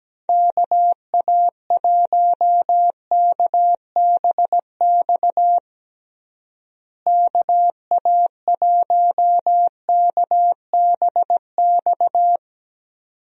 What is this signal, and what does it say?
Morse code, heard as a keyed tone, about 17 wpm, 700 Hz. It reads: KA1KBX KA1KBX
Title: USA Call Signs - 6 Characters - Repeated 2x - 17wpm